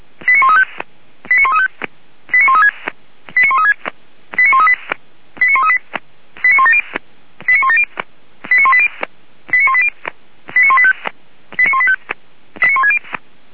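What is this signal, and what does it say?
Five-tone selective call (ZVEI), heard as quick runs of five tones, tones of about 70 ms. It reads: 89135 89135 89136 89136 89138 89138 89170 89170 89180 89180 89185 89185
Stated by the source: ZVEI five-tone call 89133 received on SDR, from a feuerwehr-orlamuende.org fire brigade audio server.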